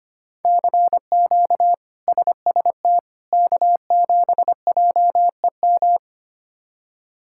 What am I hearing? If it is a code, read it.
Morse code, heard as a keyed tone, about 25 wpm, 700 Hz. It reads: CQ SST K7JEM
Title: K1USN SST QSO Practice - 25wpm